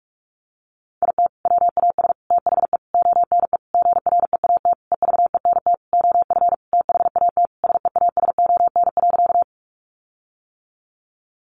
Morse code, heard as a keyed tone, about 44 wpm, 700 Hz. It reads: IT WAS THE ONE GREAT EVENT OF THAT SEASON.
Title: Sentences from Top 1400 Words 45wpm